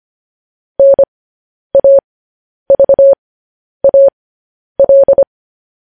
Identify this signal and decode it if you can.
Morse code, heard as a keyed tone, about 25 wpm, spaced as for 10 wpm, 550 Hz. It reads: NAVAL
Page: news-headlines-in-morse-code-at-10-wpm